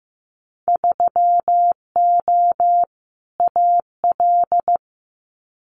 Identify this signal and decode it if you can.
Morse code, heard as a keyed tone, about 15 wpm, 700 Hz. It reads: 3O AL
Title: Winter Field Day Exchanges Encore 15wpm